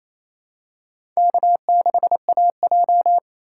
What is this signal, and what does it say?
Morse code, heard as a keyed tone, about 28 wpm, 700 Hz. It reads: K6AJ
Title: USA Call Signs - 4 Characters - 6x Character Spacing - 28wpm